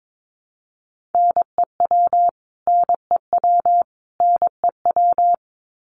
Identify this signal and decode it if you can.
Morse code, heard as a keyed tone, about 22 wpm, 700 Hz. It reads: NEW NEW NEW
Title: Top 200 Words Repeated 22wpm